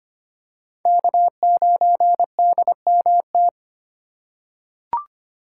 Morse code, heard as a keyed tone, about 25 wpm, 700 Hz. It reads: K9DMT
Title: USA Call Signs - 5 Characters - 2x Character Spacing - 25wpm